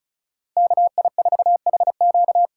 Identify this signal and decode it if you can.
Morse code, heard as a keyed tone, about 34 wpm, 700 Hz. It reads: KI4HQ
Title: USA Callsign DE Callsign 35wpm